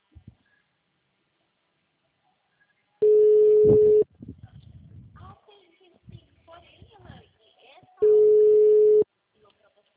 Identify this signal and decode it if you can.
Ringback tone (Europe, 425 Hz), with speech.
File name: out- -2022- - - .8912